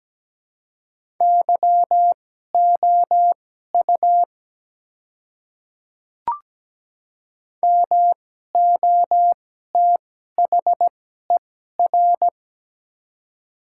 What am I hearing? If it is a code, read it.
Morse code, heard as a keyed tone, about 17 wpm, spaced as for 12 wpm, 700 Hz. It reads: YOU MOTHER
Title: 2x Character Spacing - Top 300 Words - 17wpm